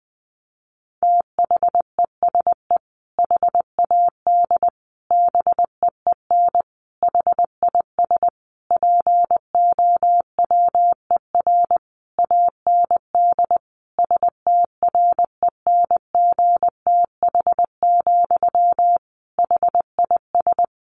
Morse code, heard as a keyed tone, about 20 wpm, 700 Hz. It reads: THESE HAD BEEN HIS POWER AND STRENGTH, HIS
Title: Sentences from Top 1500 Words 20wpm